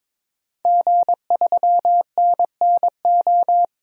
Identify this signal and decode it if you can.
Morse code, heard as a keyed tone, about 22 wpm, 700 Hz. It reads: G3NNO
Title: DXCC-100 Call Signs - 5 Characters - 22wpm